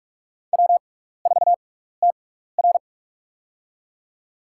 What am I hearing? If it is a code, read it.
Morse code, heard as a keyed tone, about 46 wpm, spaced as for 16 wpm, 700 Hz. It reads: W4TR
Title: USA Call Signs - 4 Characters - 6x Character Spacing - 45wpm